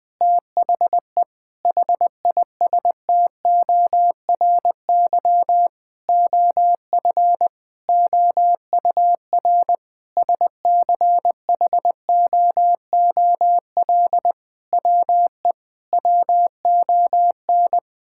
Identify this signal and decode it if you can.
Morse code, heard as a keyed tone, about 20 wpm, 700 Hz. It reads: THE HISTORY OF OUR SCHOOL WE WON
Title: Sentences with 8-Word Phrases - 20wpm